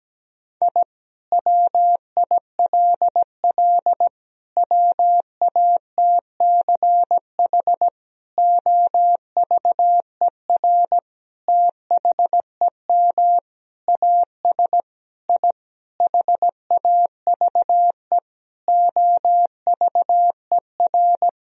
Morse code, heard as a keyed tone, about 17 wpm, 700 Hz. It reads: I WILL WATCH OVER THEM AS I HAVE OVER
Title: Sentences from Top 1000 Words - Review Part 1 - 17wpm